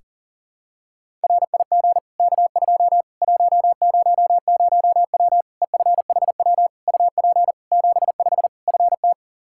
Morse code, heard as a keyed tone, about 40 wpm, 700 Hz. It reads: RIG K2 100W EFHW UP 75 FT
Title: Practice QSO - Exchange 2 - Info - 40wpm